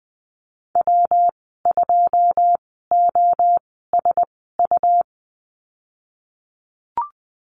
Morse code, heard as a keyed tone, about 20 wpm, spaced as for 15 wpm, 700 Hz. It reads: W2OSU
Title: 186 Lesson 30g - Letter b - Call Signs 20wpm